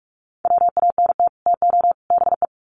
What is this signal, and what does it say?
Morse code, heard as a keyed tone, about 44 wpm, 700 Hz. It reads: WANT TO BE
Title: Sets of 3 Words 45wpm